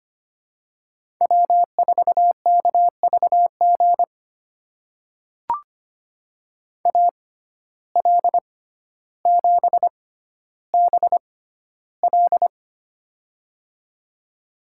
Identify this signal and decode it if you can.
Morse code, heard as a keyed tone, about 25 wpm, spaced as for 13 wpm, 690 Hz. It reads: W4KVG AL7BL
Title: USA Call Signs - 5 Characters - 6x Character Spacing - 25wpm